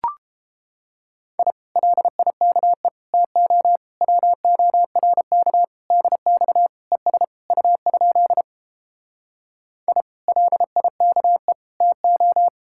Morse code, heard as a keyed tone, about 31 wpm, 700 Hz. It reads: I LIKE TO WORK DX ES U? I LIKE TO